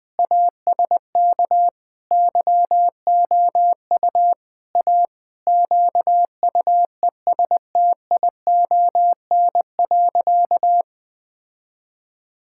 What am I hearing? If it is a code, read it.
Morse code, heard as a keyed tone, about 20 wpm, 700 Hz. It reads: ASK YOU A QUESTION.
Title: Sentences with 5-Word Phrases - 20wpm